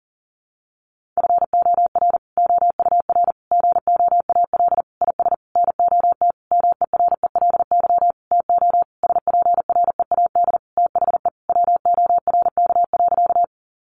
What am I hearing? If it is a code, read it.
Morse code, heard as a keyed tone, about 40 wpm, 700 Hz. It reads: FOR OUR GOAL IS NOT MERELY TO SPREAD THE WORK.
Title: Sentences from Top 1400 Words 40wpm